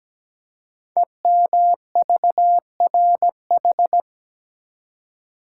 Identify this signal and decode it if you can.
Morse code, heard as a keyed tone, about 17 wpm, 700 Hz. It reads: EMVRH
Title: Code Groups - 5 Letters - 17wpm